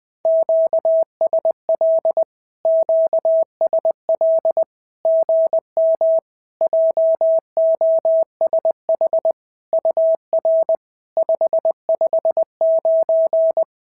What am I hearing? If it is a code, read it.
Morse code, heard as a keyed tone, about 19 wpm, 650 Hz. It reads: QSL QSL GM JOSH UR 559
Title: SOTA QSO Practice 20wpm